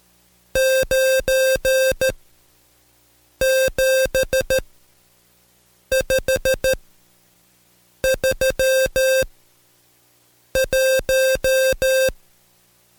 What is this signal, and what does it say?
Morse code, heard as a keyed tone, about 13 wpm, spaced as for 5 wpm, 530 Hz. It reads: 97531
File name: Code Quick / AudioDownld2 / Seg107-Day13-DayDrill 13